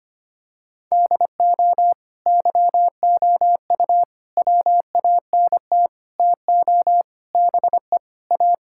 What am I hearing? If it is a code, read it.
Morse code, heard as a keyed tone, about 25 wpm, 700 Hz. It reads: DO YOU WANT TO BE A
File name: practice / 25wpm / Sentences with 4-Word Phrases - 25wpm